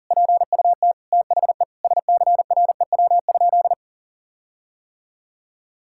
Morse code, heard as a keyed tone, about 40 wpm, 700 Hz. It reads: PUT THE SCREW?
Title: Sentences from Top 1600 Words 40wpm